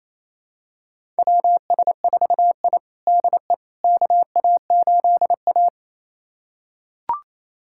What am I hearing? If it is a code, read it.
Morse code, heard as a keyed tone, about 28 wpm, 700 Hz. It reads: WS4I DE KA8A
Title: USA Callsign DE Callsign - 4 Characters - 28wpm